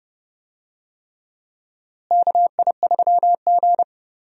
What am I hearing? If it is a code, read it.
Morse code, heard as a keyed tone, about 30 wpm, 700 Hz. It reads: KI3G
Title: USA Call Signs - 4 Characters - Repeated 3x - 30wpm